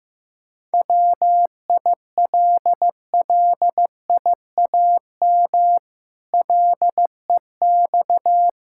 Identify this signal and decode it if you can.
Morse code, heard as a keyed tone, about 15 wpm, 700 Hz. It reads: WILLIAM LEX
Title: Pangram Challenge v2 - 15wpm